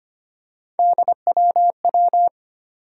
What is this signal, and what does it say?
Morse code, heard as a keyed tone, about 25 wpm, 700 Hz. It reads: DWW
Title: USA Call Sign Postfixes - 3 Letters 25wpm